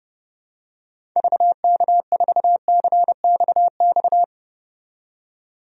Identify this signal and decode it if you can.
Morse code, heard as a keyed tone, about 32 wpm, 700 Hz. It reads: VK4CXX
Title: DXCC-100 Call Signs - 6 Characters - 30wpm